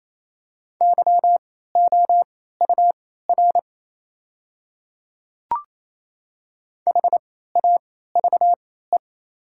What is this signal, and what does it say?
Morse code, heard as a keyed tone, about 28 wpm, spaced as for 16 wpm, 690 Hz. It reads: YOUR HAVE
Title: 3x Character Spacing - Top 100 Words - 28wpm